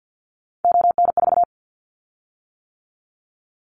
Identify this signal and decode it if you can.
Morse code, heard as a keyed tone, about 50 wpm, 700 Hz. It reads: ON4
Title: Callsign Prefixes for Top 50 DXCC - 50wpm